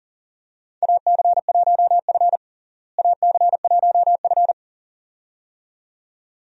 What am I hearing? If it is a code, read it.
Morse code, heard as a keyed tone, about 40 wpm, 700 Hz. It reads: AC1F AC1F
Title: USA Call Signs - 4 Characters - Repeated 2x - 40wpm